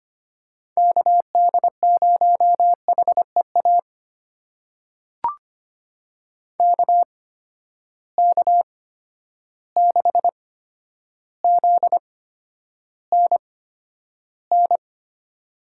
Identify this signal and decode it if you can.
Morse code, heard as a keyed tone, about 25 wpm, spaced as for 11 wpm, 700 Hz. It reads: KD0HEA KK6ZNN
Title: USA Call Signs - 6 Characters - 8x Character Spacing - 25wpm